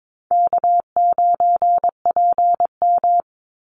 Morse code, heard as a keyed tone, about 22 wpm, 700 Hz. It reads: K9PM
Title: NA SOTA Activator Callsigns 22wpm